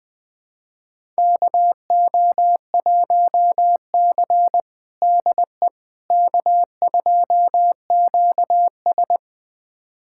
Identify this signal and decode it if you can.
Morse code, heard as a keyed tone, about 21 wpm, 700 Hz. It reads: KO1C DE K2QS
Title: USA Callsign DE Callsign - 4 Characters - 20wpm